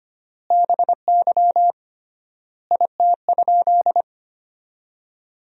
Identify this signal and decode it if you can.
Morse code, heard as a keyed tone, about 25 wpm, 700 Hz. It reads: BY IT?
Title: 3x Word-Spacing - Sentences from Top 100 Words - 25wpm